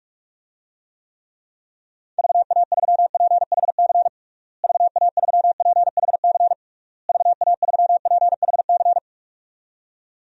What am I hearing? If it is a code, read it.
Morse code, heard as a keyed tone, about 46 wpm, 700 Hz. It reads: VA3PHC VA3PHC VA3PHC
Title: DXCC-100 Call Signs - 6 Characters - Repeated 3x - 45wpm